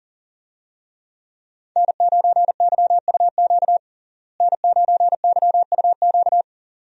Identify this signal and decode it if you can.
Morse code, heard as a keyed tone, about 37 wpm, 700 Hz. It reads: N9YUQ N9YUQ
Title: USA Call Signs - 5 Characters - Repeated 2x - 40wpm